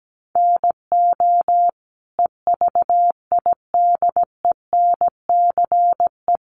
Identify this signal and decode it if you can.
Morse code, heard as a keyed tone, about 17 wpm, 700 Hz. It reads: NO EVIDENCE
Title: Sentences from Top 1300 Words 17wpm